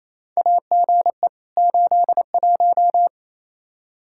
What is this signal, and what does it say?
Morse code, heard as a keyed tone, about 28 wpm, 700 Hz. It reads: AGE 81
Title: Practice QSO - Exchange 3 - More Info - 28wpm